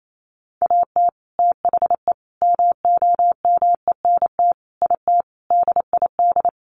Morse code, heard as a keyed tone, about 28 wpm, 700 Hz. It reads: AT THE MOMENT IT DID
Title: Sentences from Top 1200 Words 28wpm